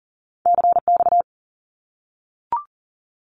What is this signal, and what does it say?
Morse code, heard as a keyed tone, about 40 wpm, 700 Hz. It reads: CX
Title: Abbreviations 40wpm